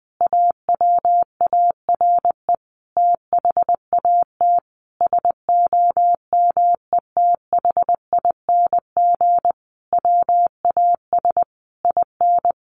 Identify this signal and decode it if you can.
Morse code, heard as a keyed tone, about 20 wpm, 700 Hz. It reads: AWARE THAT SOMETHING WAS IN
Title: Sentences from Top 1400 Words 20wpm